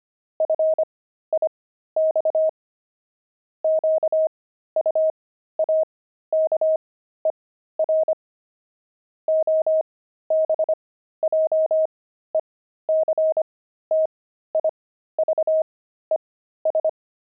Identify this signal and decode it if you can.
Morse code, heard as a keyed tone, about 25 wpm, spaced as for 13 wpm, 630 Hz. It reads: FIX QUAKER OBJECTIVES
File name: pangram-fs-13-cs-25